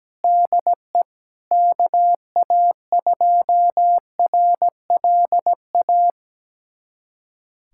Morse code, heard as a keyed tone, about 17 wpm, 700 Hz. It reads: DE KA2RLA
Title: USA Callsign DE Callsign 17wpm